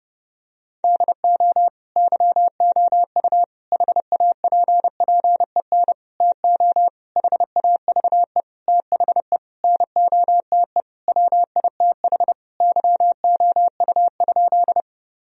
Morse code, heard as a keyed tone, about 32 wpm, 700 Hz. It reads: DO YOU HAPPEN TO HAVE THE NOTE WITH YOU?
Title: Sentences from Top 1000 Words - Review Part 2 - 30wpm